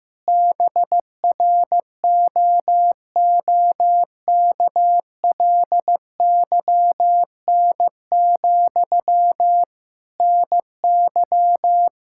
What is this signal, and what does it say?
Morse code, heard as a keyed tone, about 15 wpm, 700 Hz. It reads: BROOKLYN, NY
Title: Practice QSO - Exchange 1 - Beginning - 15wpm